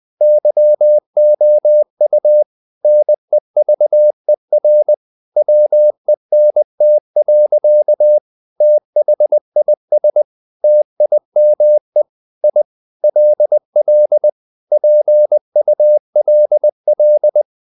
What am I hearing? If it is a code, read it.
Morse code, heard as a keyed tone, about 20 wpm, 600 Hz. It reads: YOU NEVER WENT. THIS TIME I LL PULL
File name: the old man and the sea part 2 20wpm